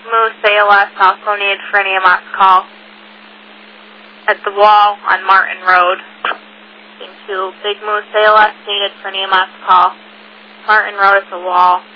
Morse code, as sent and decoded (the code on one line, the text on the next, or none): none